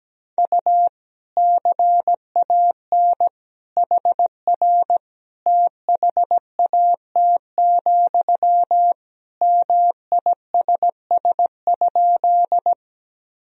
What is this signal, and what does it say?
Morse code, heard as a keyed tone, about 17 wpm, 700 Hz. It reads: U CAN HR THAT, MISS?
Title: Sentences from Top 500 Words and Abbreviations 17wpm